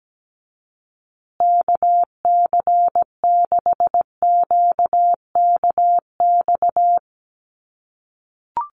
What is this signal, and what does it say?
Morse code, heard as a keyed tone, about 17 wpm, 700 Hz. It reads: KC6QKX
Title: USA Call Signs Repeated 3x 17wpm